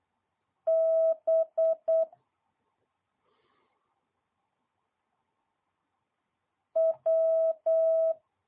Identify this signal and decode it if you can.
Morse code, heard as a keyed tone, about 8 wpm, 650 Hz. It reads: B W
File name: jh8cbh / 2022cw